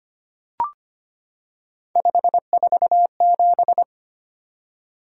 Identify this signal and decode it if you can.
Morse code, heard as a keyed tone, about 25 wpm, 700 Hz. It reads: 547
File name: Three Letter-Number Combinations - 25wpm